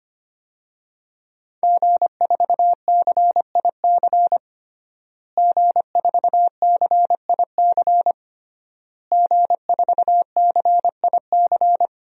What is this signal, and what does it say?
Morse code, heard as a keyed tone, about 25 wpm, 700 Hz. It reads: G4CIC G4CIC G4CIC
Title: DXCC-100 Call Signs - 5 Characters - Repeated 3x - 25wpm